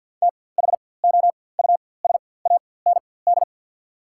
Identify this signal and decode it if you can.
Morse code, heard as a keyed tone, about 53 wpm, spaced as for 24 wpm, 690 Hz. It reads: THOUSAND
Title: 4x Character Spacing - Top 500 Words - 50wpm